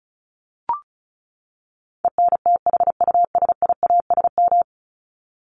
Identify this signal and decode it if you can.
Morse code, heard as a keyed tone, about 34 wpm, 700 Hz. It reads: ENTHUSIASM
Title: Top 1751-2000 Words 35wpm